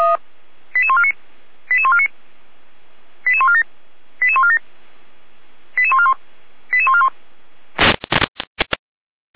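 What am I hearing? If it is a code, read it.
Five-tone selective call (ZVEI), heard as quick runs of five tones, tones of about 70 ms. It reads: 88149 88149 88147 88147 88141 88141
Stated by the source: Received on SDR as a ZVEI five-tone selective call.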